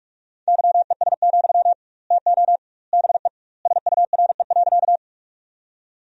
Morse code, heard as a keyed tone, about 45 wpm, 700 Hz. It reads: YES, TO BE SURE.
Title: 2x Word-Spacing - Sentences from Top 500 Words - 45wpm